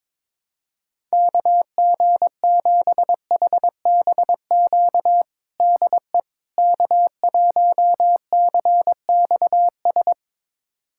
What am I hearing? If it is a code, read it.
Morse code, heard as a keyed tone, about 22 wpm, 700 Hz. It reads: KG7HBQ DE K1CXS
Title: USA Callsign DE Callsign 22wpm